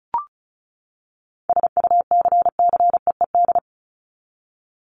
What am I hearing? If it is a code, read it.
Morse code, heard as a keyed tone, about 34 wpm, 700 Hz. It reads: SUCCEED
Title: Top 1501-1750 Words 35wpm